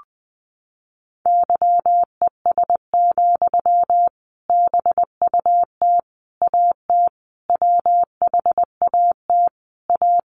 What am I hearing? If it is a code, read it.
Morse code, heard as a keyed tone, about 20 wpm, 700 Hz. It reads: YES, BUT AT WHAT A
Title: Sentences from Top 800 Words 20wpm